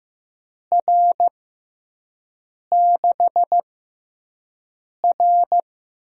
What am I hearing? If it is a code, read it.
Morse code, heard as a keyed tone, about 16 wpm, spaced as for 5 wpm, 700 Hz. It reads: R6R
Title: USA Call Signs - 4 Characters - 6x Character Spacing - 15wpm